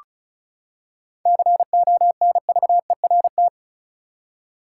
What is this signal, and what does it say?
Morse code, heard as a keyed tone, about 36 wpm, 700 Hz. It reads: CONVERT
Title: Seven Letter Words - 35wpm